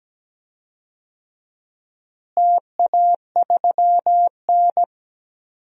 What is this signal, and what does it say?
Morse code, heard as a keyed tone, about 17 wpm, 700 Hz. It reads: TA3N
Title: DXCC-100 Call Signs - 4 Characters - Repeated 2x - 17wpm